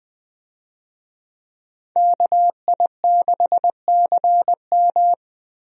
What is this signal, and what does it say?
Morse code, heard as a keyed tone, about 20 wpm, 700 Hz. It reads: KI6CM
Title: USA Call Signs - 5 Characters - Repeated 3x - 20wpm